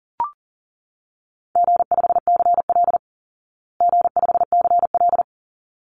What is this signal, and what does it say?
Morse code, heard as a keyed tone, about 40 wpm, 700 Hz. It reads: G5CL G5CL
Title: Easy Call Signs Repeated 2x - 40wpm